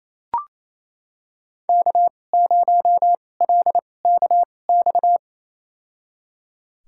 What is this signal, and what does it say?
Morse code, heard as a keyed tone, about 27 wpm, spaced as for 20 wpm, 700 Hz. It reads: K0LKX
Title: USA Call Signs - 5 Characters - 2x Character Spacing - 28wpm